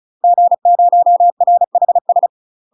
Morse code, heard as a keyed tone, about 34 wpm, 700 Hz. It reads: G0RHS
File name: UK MASTER QSOs TRAINING / 35WPM / UK MASTER QSOs 0000